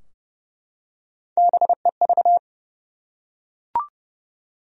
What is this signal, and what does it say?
Morse code, heard as a keyed tone, about 30 wpm, 700 Hz. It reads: BEV
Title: USA Common Names - Max 3 Letters - 30wpm